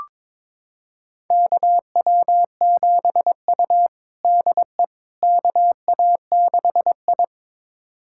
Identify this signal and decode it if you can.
Morse code, heard as a keyed tone, about 22 wpm, 700 Hz. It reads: KW7U DE KA6I
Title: USA Callsign DE Callsign - 4 Characters - 22wpm